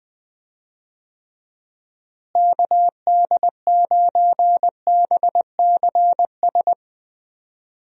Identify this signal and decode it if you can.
Morse code, heard as a keyed tone, about 20 wpm, 700 Hz. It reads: KD9BCS
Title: USA Call Signs - 6 Characters - Repeated 3x - 20wpm